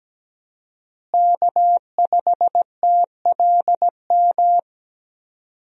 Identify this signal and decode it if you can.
Morse code, heard as a keyed tone, about 17 wpm, 700 Hz. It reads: K5TLM
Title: USA Call Signs - 5 Characters - Repeated 3x - 17wpm